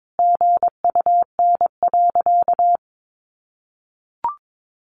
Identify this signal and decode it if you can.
Morse code, heard as a keyed tone, about 22 wpm, 700 Hz. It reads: GUN.
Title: A Sound Of Thunder 22wpm